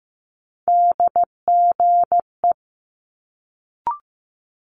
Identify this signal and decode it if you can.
Morse code, heard as a keyed tone, about 15 wpm, 700 Hz. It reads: DGE
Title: Multi-letter Phonemes Quick Fire 15wpm